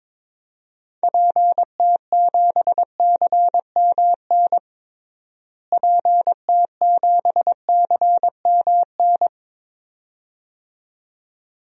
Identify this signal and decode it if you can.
Morse code, heard as a keyed tone, about 22 wpm, 700 Hz. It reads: PT7CMN PT7CMN
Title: DXCC-100 Call Signs - 6 Characters - Repeated 2x - 22wpm